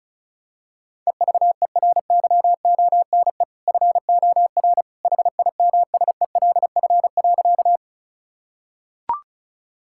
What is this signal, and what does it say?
Morse code, heard as a keyed tone, about 36 wpm, 690 Hz. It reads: EVERYONE FOR HIMSELF.
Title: Sentences from Top 1600 Words 35wpm